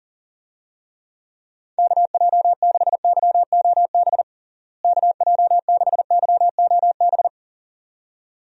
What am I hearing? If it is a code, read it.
Morse code, heard as a keyed tone, about 40 wpm, 700 Hz. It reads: KJ6YOB KJ6YOB
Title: USA Call Signs - 6 Characters - Repeated 2x - 40wpm